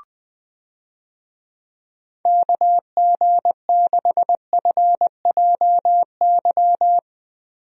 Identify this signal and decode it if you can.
Morse code, heard as a keyed tone, about 20 wpm, 700 Hz. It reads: KG6FJY
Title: USA Call Signs - 6 Characters - 20wpm